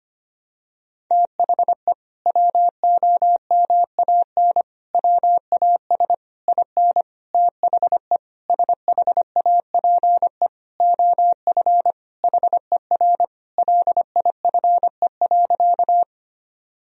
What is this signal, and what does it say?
Morse code, heard as a keyed tone, about 25 wpm, 700 Hz. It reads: THE WOMAN WAS IN THE SHAPE OF HER LIFE.
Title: Sentences from Top 1000 Words - Review Part 3 - 25wpm